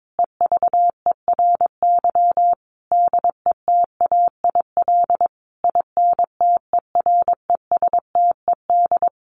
Morse code, heard as a keyed tone, about 22 wpm, 700 Hz. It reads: EVERY DETAIL INTERESTED
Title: Sentences from Top 1500 Words 22wpm